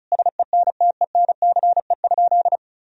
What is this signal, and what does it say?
Morse code, heard as a keyed tone, about 36 wpm, 700 Hz. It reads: SENTENCE?
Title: Sentences from Top 1000 Words - Review Part 1 - 35wpm